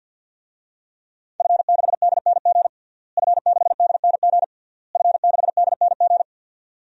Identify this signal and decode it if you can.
Morse code, heard as a keyed tone, about 52 wpm, 700 Hz. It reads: F6DNG F6DNG F6DNG
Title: DXCC-100 Call Signs - 5 Characters - Repeated 3x - 50wpm